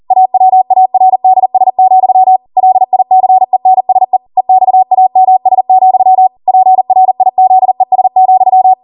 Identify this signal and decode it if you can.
Morse code, heard as a keyed tone, about 40 wpm, 750 Hz. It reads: AWARDS, LICENSE EXAMS, PRIZES,